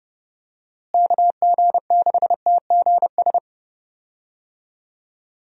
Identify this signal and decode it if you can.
Morse code, heard as a keyed tone, about 30 wpm, 700 Hz. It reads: KG6TGS
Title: USA Call Signs - 6 Characters - Repeated 2x - 30wpm